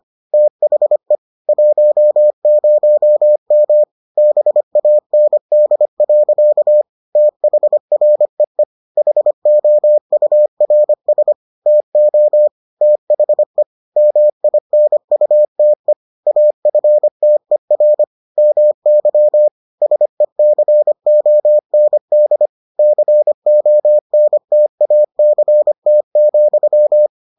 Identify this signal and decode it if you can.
Morse code, heard as a keyed tone, about 25 wpm, 600 Hz. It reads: THE 10M BAND. THREE HOURS TO THE MINUTE AFTER MY SECOND CONTACT,